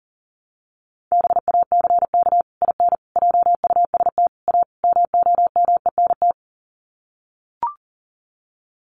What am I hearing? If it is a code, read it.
Morse code, heard as a keyed tone, about 40 wpm, 700 Hz. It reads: BACK IN JUST A MOMENT
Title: Sets of 5 Words Encore 40wpm